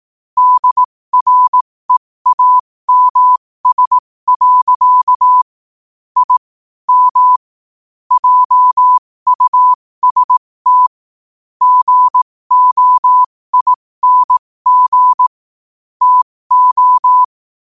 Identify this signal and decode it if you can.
Morse code, heard as a keyed tone, about 18 wpm, spaced as for 15 wpm, 1000 Hz. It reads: DREAMS. I M JUST GOING TO
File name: qotdfiles / 15wpm